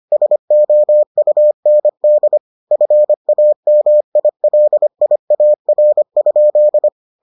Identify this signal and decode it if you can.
Morse code, heard as a keyed tone, about 25 wpm, 600 Hz. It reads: SOUND FAMILIAR?